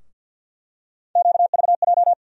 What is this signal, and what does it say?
Morse code, heard as a keyed tone, about 47 wpm, 690 Hz. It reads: QVJ
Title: USA Call Sign Postfixes - 3 Letters 50wpm